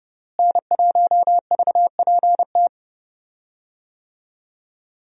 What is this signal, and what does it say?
Morse code, heard as a keyed tone, about 30 wpm, 700 Hz. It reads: N1VPT
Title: USA Call Signs - 5 Characters - Repeated 3x - 30wpm